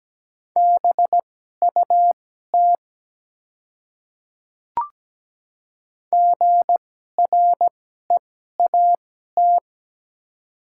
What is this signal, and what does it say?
Morse code, heard as a keyed tone, about 17 wpm, spaced as for 12 wpm, 690 Hz. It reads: BUT GREAT
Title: 2x Character Spacing - Top 300 Words - 17wpm